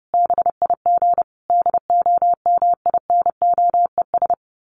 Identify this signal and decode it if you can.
Morse code, heard as a keyed tone, about 30 wpm, 700 Hz. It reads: BIG DOMINOES
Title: A Sound Of Thunder 30wpm